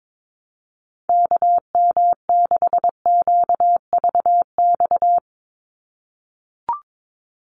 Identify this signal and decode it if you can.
Morse code, heard as a keyed tone, about 22 wpm, 700 Hz. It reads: KM6QVX
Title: USA Call Signs 22wpm